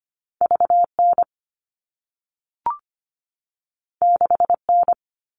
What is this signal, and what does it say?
Morse code, heard as a keyed tone, about 25 wpm, 700 Hz. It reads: VN 6N
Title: All Cut Numbers - 2-Digit 25wpm